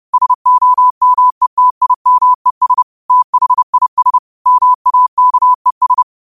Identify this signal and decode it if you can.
Morse code, heard as a keyed tone, about 30 wpm, 1000 Hz. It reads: SOMETIMES THIS MAKES
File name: qotdfiles / 30wpm